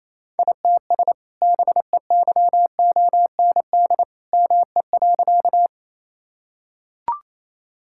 Morse code, heard as a keyed tone, about 28 wpm, 700 Hz. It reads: ITS BEYOND ME.
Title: Sentences from Top 1000 Words - Review Part 6 - 28wpm